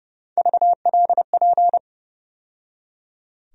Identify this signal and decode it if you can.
Morse code, heard as a keyed tone, about 30 wpm, 700 Hz. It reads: VLP
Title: USA Call Sign Postfixes - 3 Letters 30wpm